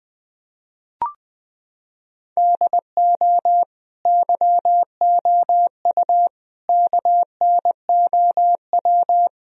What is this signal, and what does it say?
Morse code, heard as a keyed tone, about 20 wpm, 700 Hz. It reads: DO YOU KNOW
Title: Sentences from Top 1000 Words - Review Part 2 - 20wpm